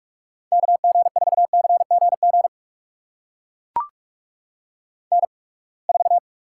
Morse code, heard as a keyed tone, about 44 wpm, spaced as for 31 wpm, 700 Hz. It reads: KG4CGG N4
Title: USA Call Signs - 8x Character Spacing - 45wpm